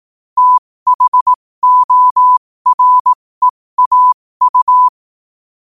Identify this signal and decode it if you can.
Morse code, heard as a keyed tone, about 18 wpm, spaced as for 15 wpm, 1000 Hz. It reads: THOREAU
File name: qotdfiles / 15wpm